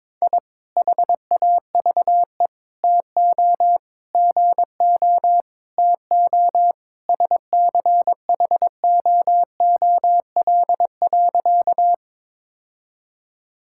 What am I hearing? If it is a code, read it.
Morse code, heard as a keyed tone, about 22 wpm, 700 Hz. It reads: I HAVE TO GO TO SCHOOL.
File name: practice / 22wpm / Sentences with 3-Word Phrases - 22wpm